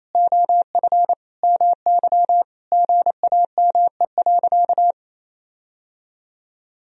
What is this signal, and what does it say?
Morse code, heard as a keyed tone, about 28 wpm, 700 Hz. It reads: OF MY GAME.
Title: Sentences from Top 1000 Words - Review Part 2 - 28wpm